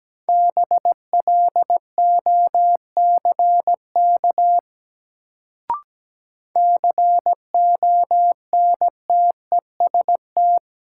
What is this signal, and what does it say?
Morse code, heard as a keyed tone, about 17 wpm, 700 Hz. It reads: BLOCK CONTEST
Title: Sentences from Top 1600 Words - New Word Drill - 17wpm